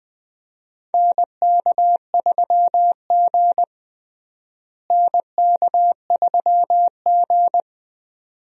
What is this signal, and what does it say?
Morse code, heard as a keyed tone, about 20 wpm, 700 Hz. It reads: NK3G NK3G
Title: USA Call Signs - 4 Characters - Repeated 2x - 20wpm